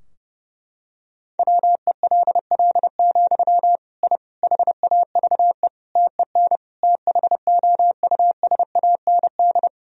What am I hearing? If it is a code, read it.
Morse code, heard as a keyed tone, about 32 wpm, 700 Hz. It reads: WELL, I HAVE TEN THOUSAND
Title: Sentences from Top 1000 Words - Review Part 3 - 30wpm